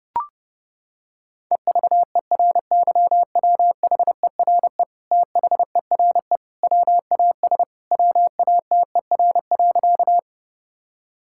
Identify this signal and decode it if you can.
Morse code, heard as a keyed tone, about 30 wpm, 700 Hz. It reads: EVERYWHERE THERE WAS WATER.
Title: Sentences from Top 1600 Words 30wpm